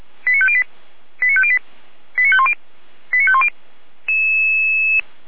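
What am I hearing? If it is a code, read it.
Five-tone selective call (ZVEI), heard as quick runs of five tones, tones of about 70 ms. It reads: 89508 89508 89510 89510
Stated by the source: ZVEI five-tone selective call received by SDR, from the Feuerwehr Orlamünde fire brigade audio archive.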